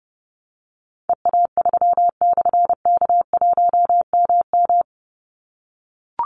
Morse code, heard as a keyed tone, about 30 wpm, 700 Hz. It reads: EA3/K1MM
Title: Call Signs - 30wpm